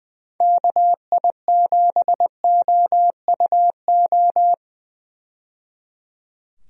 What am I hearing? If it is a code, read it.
Morse code, heard as a keyed tone, about 20 wpm, 700 Hz. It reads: KI7OUO
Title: USA Call Signs - 6 Characters - 20wpm